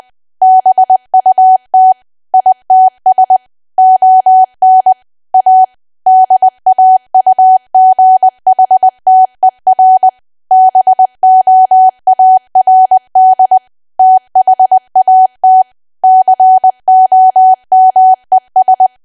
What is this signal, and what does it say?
Morse code, heard as a keyed tone, about 20 wpm, 750 Hz. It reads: BUT ITS ON A DAUGHTER BOARD THAT COMES
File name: cwa / W1AW-Level3 / 20WPM